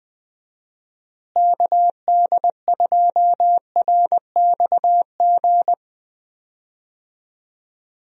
Morse code, heard as a keyed tone, about 20 wpm, 700 Hz. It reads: KD2RXG
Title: USA Call Signs - 6 Characters - Repeated 2x - 20wpm